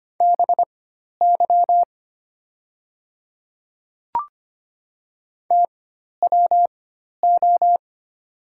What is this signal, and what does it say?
Morse code, heard as a keyed tone, about 25 wpm, spaced as for 12 wpm, 700 Hz. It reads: BY TWO
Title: 4x Character Spacing - Top 100 Words - 25wpm